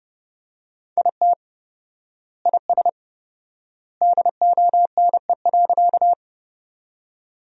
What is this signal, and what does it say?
Morse code, heard as a keyed tone, about 30 wpm, 700 Hz. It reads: IT IS DONE.
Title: 4x Word-Spacing - Sentences from Top 500 Words - 30wpm